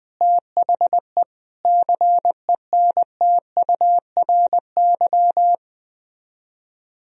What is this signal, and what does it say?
Morse code, heard as a keyed tone, about 20 wpm, 700 Hz. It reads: THE CENTURY
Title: Speed-Racing - Sets of 5 Words 17wpm